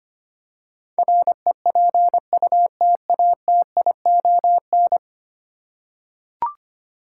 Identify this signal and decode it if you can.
Morse code, heard as a keyed tone, about 25 wpm, 690 Hz. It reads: REPUTATION
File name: 10 Letter Words - First 3 Letters Given - 25wpm